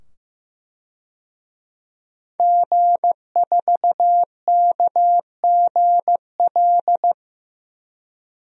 Morse code, heard as a keyed tone, about 15 wpm, 700 Hz. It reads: G4KGL
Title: DXCC-100 Call Signs - 5 Characters - 15wpm